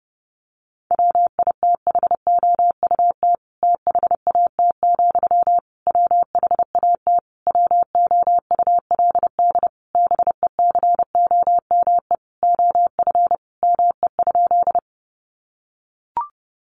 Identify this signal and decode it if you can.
Morse code, heard as a keyed tone, about 30 wpm, 700 Hz. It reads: WITHOUT THAT, WHAT WOULD BECOME OF ME?
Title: Sentences from Top 600 Words 30wpm